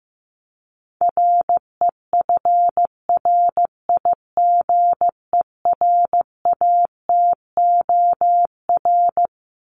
Morse code, heard as a keyed tone, about 15 wpm, 700 Hz. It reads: REFRIGERATOR
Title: Top 1501-1750 Words 15wpm